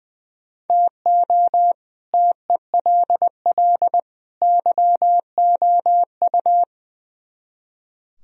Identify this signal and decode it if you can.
Morse code, heard as a keyed tone, about 23 wpm, 700 Hz. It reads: TO TELL YOU
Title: Speed-Racing - Sets of 3 Words 20wpm